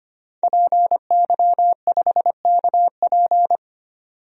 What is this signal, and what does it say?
Morse code, heard as a keyed tone, about 25 wpm, 700 Hz. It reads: PY5KP
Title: DXCC-100 Call Signs - 5 Characters - 25wpm